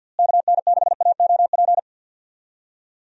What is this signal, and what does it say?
Morse code, heard as a keyed tone, about 51 wpm, 700 Hz. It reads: KN6AOP
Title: USA Call Signs - 6 Characters - Repeated 2x - 50wpm